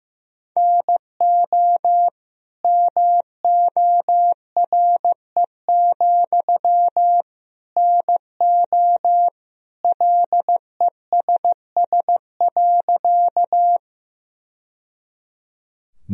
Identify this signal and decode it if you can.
Morse code, heard as a keyed tone, about 16 wpm, 700 Hz. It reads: NO MORE, NO LESS.
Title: Sentences from Top 1000 Words - Review Part 2 - 15wpm